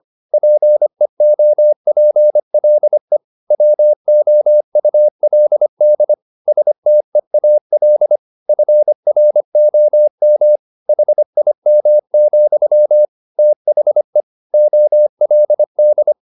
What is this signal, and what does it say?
Morse code, heard as a keyed tone, about 25 wpm, 600 Hz. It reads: PEOPLE WOULD STEAL FROM HIM, THE OLD
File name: the old man and the sea part 1 25wpm